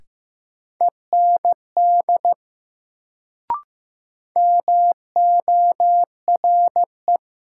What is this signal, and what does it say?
Morse code, heard as a keyed tone, about 15 wpm, 700 Hz. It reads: END MORE
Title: Sets of 2 Words - New Word Drill - 15wpm